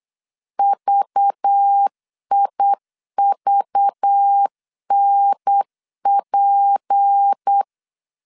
Morse code, heard as a keyed tone, about 8 wpm, 800 Hz. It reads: VIVNP
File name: LEZIONI / 6 PKX0 / 6 P 8wpm 3025